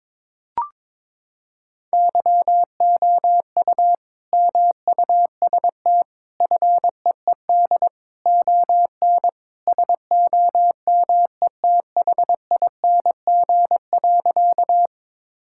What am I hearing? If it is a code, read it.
Morse code, heard as a keyed tone, about 22 wpm, 700 Hz. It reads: YOU MUST FEED ON SOMETHING.
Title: Sentences from Top 1600 Words 22wpm